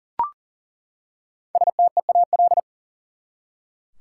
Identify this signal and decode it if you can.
Morse code, heard as a keyed tone, about 38 wpm, 700 Hz. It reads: STEAL
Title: Five Letter Words - 40wpm